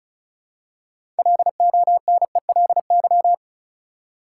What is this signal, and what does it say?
Morse code, heard as a keyed tone, about 36 wpm, 700 Hz. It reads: LONELY
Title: Six Letter Words - 35wpm